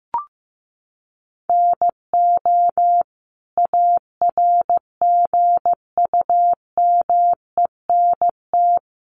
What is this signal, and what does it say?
Morse code, heard as a keyed tone, about 15 wpm, 700 Hz. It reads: NO ARGUMENT
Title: Sentences from Top 1400 Words 15wpm